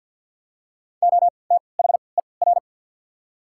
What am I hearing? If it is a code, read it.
Morse code, heard as a keyed tone, about 53 wpm, spaced as for 29 wpm, 700 Hz. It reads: OTHER
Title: 3x Character Spacing - Top 300 Words - 50wpm